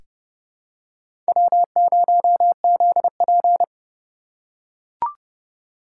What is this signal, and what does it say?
Morse code, heard as a keyed tone, about 30 wpm, 700 Hz. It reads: W0ZP
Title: K1USN SST QSO Practice - 30wpm